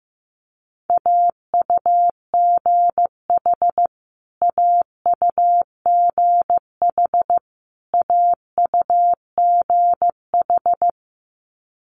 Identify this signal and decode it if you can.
Morse code, heard as a keyed tone, about 15 wpm, 700 Hz. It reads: AUGH AUGH AUGH
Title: Multi-letter Phonemes Repeated 3x 15wpm